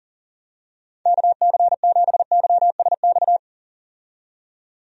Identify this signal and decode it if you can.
Morse code, heard as a keyed tone, about 37 wpm, 700 Hz. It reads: KC7YSX
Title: USA Call Signs - 6 Characters - 40wpm